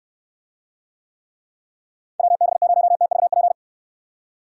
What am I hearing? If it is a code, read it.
Morse code, heard as a keyed tone, about 69 wpm, 700 Hz. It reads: WB0TVP